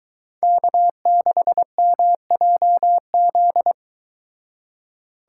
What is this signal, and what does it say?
Morse code, heard as a keyed tone, about 23 wpm, 700 Hz. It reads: K6MJZ